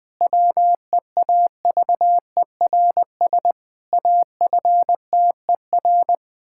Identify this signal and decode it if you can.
Morse code, heard as a keyed tone, about 21 wpm, 700 Hz. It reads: WEAVERS AFTER
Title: Pangram Challenge v2 - 20wpm